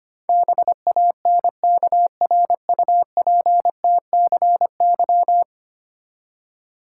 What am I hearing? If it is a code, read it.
Morse code, heard as a keyed tone, about 25 wpm, 700 Hz. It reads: BANKRUPTCY
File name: practice / 25wpm / Ten Letter Words - 25wpm